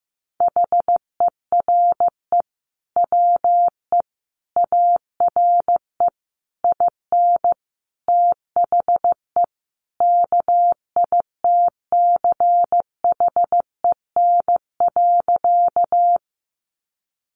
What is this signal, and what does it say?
Morse code, heard as a keyed tone, about 15 wpm, 700 Hz. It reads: HERE WE ARE IN THE KITCHEN.
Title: Sentences from Top 1400 Words 15wpm